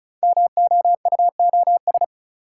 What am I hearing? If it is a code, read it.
Morse code, heard as a keyed tone, about 36 wpm, 690 Hz. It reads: MOUOS
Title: Code Groups - 5 Letters - 35wpm